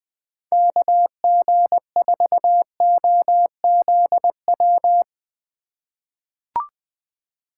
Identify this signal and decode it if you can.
Morse code, heard as a keyed tone, about 20 wpm, 700 Hz. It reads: KG4OZW
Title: USA Call Signs - 6 Characters - 4x Character Spacing - 20wpm